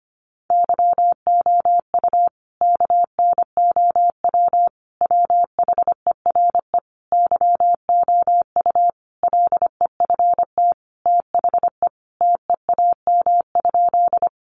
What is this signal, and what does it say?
Morse code, heard as a keyed tone, about 25 wpm, 700 Hz. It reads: YOU KNOW WHERE YOU LEFT THE TEAM?
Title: Sentences from Top 1500 Words 25wpm